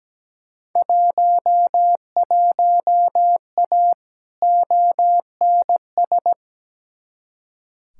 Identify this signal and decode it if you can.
Morse code, heard as a keyed tone, about 17 wpm, 700 Hz. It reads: 11A ONS
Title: Field Day Exchanges Encore - 17wpm